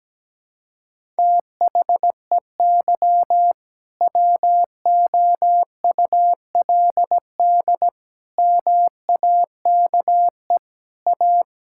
Morse code, heard as a keyed tone, about 17 wpm, 700 Hz. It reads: THEY WOULD MAKE A
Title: Sentences from Top 1600 Words 17wpm